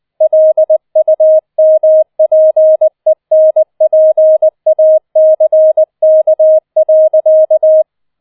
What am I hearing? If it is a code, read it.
Morse code, heard as a keyed tone, about 20 wpm, 630 Hz. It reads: LUMPENPACK.
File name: 007 Letzter Streich 20wpm